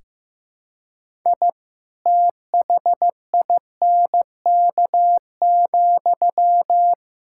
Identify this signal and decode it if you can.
Morse code, heard as a keyed tone, about 15 wpm, 700 Hz. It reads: I THINK,
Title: Sentences from Top 1000 Words - Review Part 5 - 15wpm